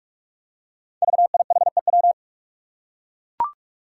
Morse code, heard as a keyed tone, about 44 wpm, 700 Hz. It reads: VIHEW
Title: Code Groups - 5 Letter:Numbers - 45wpm